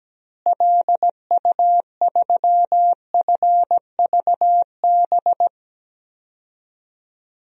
Morse code, heard as a keyed tone, about 17 wpm, 700 Hz. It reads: LU3FVB
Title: DXCC-100 Call Signs - 6 Characters - Repeated 3x - 17wpm